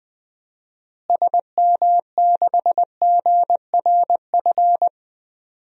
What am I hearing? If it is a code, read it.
Morse code, heard as a keyed tone, about 20 wpm, 700 Hz. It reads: SM6GRF
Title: DXCC-100 Call Signs - 6 Characters - Repeated 2x - 20wpm